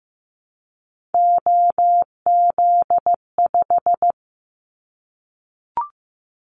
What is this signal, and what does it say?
Morse code, heard as a keyed tone, about 15 wpm, 700 Hz. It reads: OZ5
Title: Callsign Prefixes for Top 50 DXCC - 15wpm